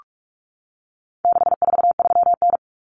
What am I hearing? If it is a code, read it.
Morse code, heard as a keyed tone, about 44 wpm, 700 Hz. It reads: 643N
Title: Common Cut Numbers - 4-Digit 45wpm